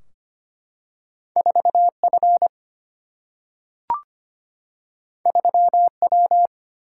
Morse code, heard as a keyed tone, about 25 wpm, 700 Hz. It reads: 4 F 3 W
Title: Single-Letter Single-Number 25wpm